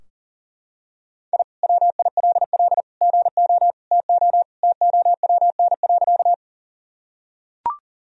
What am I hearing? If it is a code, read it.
Morse code, heard as a keyed tone, about 40 wpm, 700 Hz. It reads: I WILL GO TO TOWN.
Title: Sentences from Top 1000 Words - Review Part 2 - 40wpm